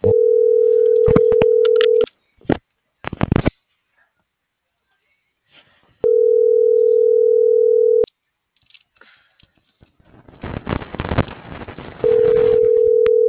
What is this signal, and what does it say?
Ringback tone (North America), 440 + 480 Hz.